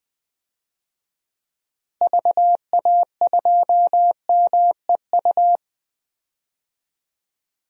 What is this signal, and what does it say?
Morse code, heard as a keyed tone, about 20 wpm, 700 Hz. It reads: VA2MEU
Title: DXCC-100 Call Signs - 6 Characters - Repeated 3x - 20wpm